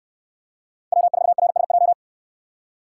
Morse code, heard as a keyed tone, about 64 wpm, 700 Hz. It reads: W4RSJ